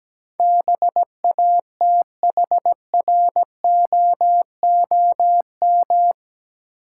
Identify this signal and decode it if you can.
Morse code, heard as a keyed tone, about 17 wpm, 700 Hz. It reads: BATHROOM
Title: Sets of 5 Words Encore - New Word Drill - 17wpm